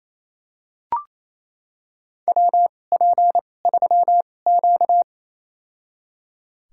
Morse code, heard as a keyed tone, about 28 wpm, spaced as for 20 wpm, 700 Hz. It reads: WP3Q
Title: USA Call Signs - 4 Characters - 2x Character Spacing - 28wpm